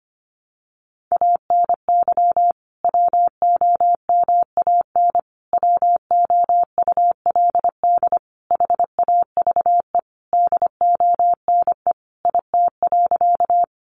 Morse code, heard as a keyed tone, about 25 wpm, 700 Hz. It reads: ANY WOMAN WOULD HAVE DONE IT.
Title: Sentences from Top 1100 Words 25wpm